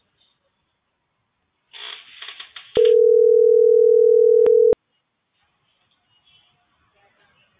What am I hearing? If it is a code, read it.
Ringback tone (North America), 440 + 480 Hz.